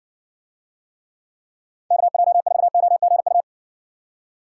Morse code, heard as a keyed tone, about 60 wpm, 700 Hz. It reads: KJ4OGU